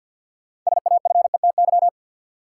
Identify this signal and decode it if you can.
Morse code, heard as a keyed tone, about 47 wpm, 700 Hz. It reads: SAFETY
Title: Sentences from Top 700 Words - New Word Drill - 50wpm